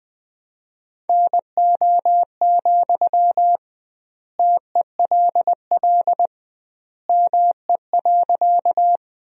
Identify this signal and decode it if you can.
Morse code, heard as a keyed tone, about 20 wpm, 700 Hz. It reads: NO, TELL ME.
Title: 2x Word-Spacing - Sentences from Top 200 Words - 20wpm